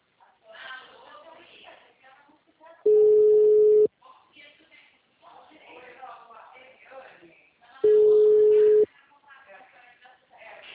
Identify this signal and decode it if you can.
Ringback tone (Europe, 425 Hz), with speech.